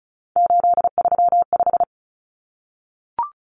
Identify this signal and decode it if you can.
Morse code, heard as a keyed tone, about 34 wpm, 700 Hz. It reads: 835
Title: Numbers - 3-Digit 35wpm